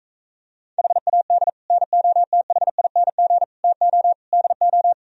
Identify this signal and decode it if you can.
Morse code, heard as a keyed tone, about 45 wpm, 700 Hz. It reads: HAD NOTHING TO DO